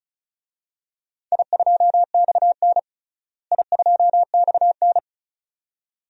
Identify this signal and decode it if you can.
Morse code, heard as a keyed tone, about 36 wpm, 700 Hz. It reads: I2XN I2XN
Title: DXCC-100 Call Signs - 4 Characters - Repeated 2x - 35wpm